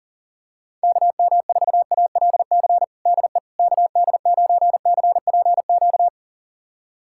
Morse code, heard as a keyed tone, about 37 wpm, 700 Hz. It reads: KM4ALC DE KD9CPQ
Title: USA Callsign DE Callsign 40wpm